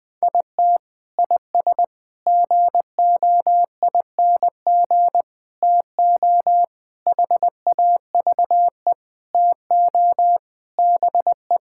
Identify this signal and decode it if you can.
Morse code, heard as a keyed tone, about 20 wpm, 700 Hz. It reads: IT IS GOING TO HAVE TO BE
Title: Sentences with 5-Word Phrases - 20wpm